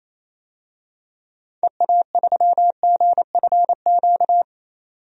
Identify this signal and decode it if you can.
Morse code, heard as a keyed tone, about 28 wpm, 700 Hz. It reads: EA3GFQ
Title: DXCC-100 Call Signs - 6 Characters - 28wpm